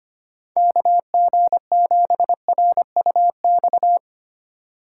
Morse code, heard as a keyed tone, about 25 wpm, 700 Hz. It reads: KG7RUX
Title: USA Call Signs - 6 Characters - 2x Character Spacing - 25wpm